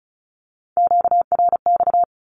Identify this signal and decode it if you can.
Morse code, heard as a keyed tone, about 36 wpm, 700 Hz. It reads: QRX
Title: Q-Codes with Spoken Meaning - 35wpm